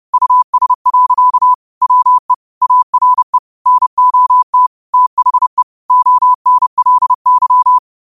Morse code, heard as a keyed tone, about 30 wpm, 1000 Hz. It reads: US. WE ARE NOT THE ONLY
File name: qotdfiles / 30wpm